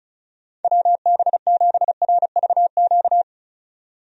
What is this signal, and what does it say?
Morse code, heard as a keyed tone, about 35 wpm, 700 Hz. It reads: WB7RVQ
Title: USA Call Signs - 6 Characters - 35wpm